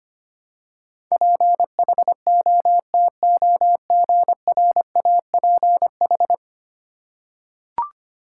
Morse code, heard as a keyed tone, about 25 wpm, 700 Hz. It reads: PHOTOGRAPH
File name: 10 Letter Words - First 3 Letters Given - 25wpm